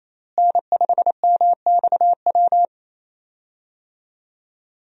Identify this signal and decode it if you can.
Morse code, heard as a keyed tone, about 28 wpm, 700 Hz. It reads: N5MXW
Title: USA Call Signs - 5 Characters - Repeated 3x - 28wpm